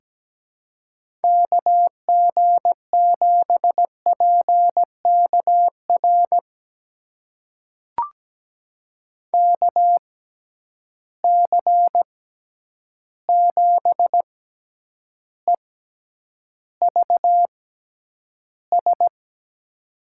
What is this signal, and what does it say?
Morse code, heard as a keyed tone, about 17 wpm, spaced as for 9 wpm, 700 Hz. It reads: KG7PKR KC7EVS
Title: USA Call Signs - 6 Characters - 6x Character Spacing - 17wpm